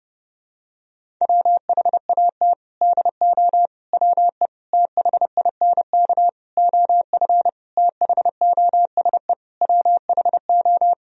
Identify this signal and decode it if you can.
Morse code, heard as a keyed tone, about 30 wpm, 700 Hz. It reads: WHAT DO WE THINK OF THOSE WHO
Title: Sentences from Top 1000 Words - Review Part 2 - 30wpm